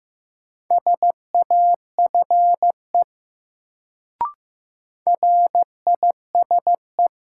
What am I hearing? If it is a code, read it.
Morse code, heard as a keyed tone, about 16 wpm, 700 Hz. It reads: SAFE RISE
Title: Sentences from Top 900 Words - New Word Drill - 15wpm